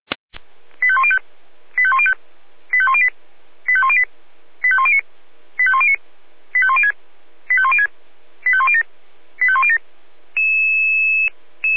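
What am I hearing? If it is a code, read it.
Five-tone selective call (ZVEI), heard as quick runs of five tones, tones of about 70 ms. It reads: 85105 85105 85108 85108 85109 85109 85106 85106 85107 85107
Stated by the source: ZVEI five-tone call received on SDR.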